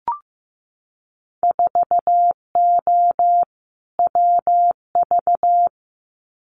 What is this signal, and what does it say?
Morse code, heard as a keyed tone, about 15 wpm, 700 Hz. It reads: 4O WV
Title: Winter Field Day Exchanges Encore 15wpm